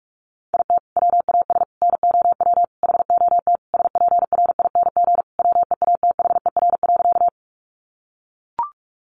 Morse code, heard as a keyed tone, about 44 wpm, 700 Hz. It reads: IT WAS NOW HOT SPRING WEATHER.
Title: Sentences from Top 1200 Words 45wpm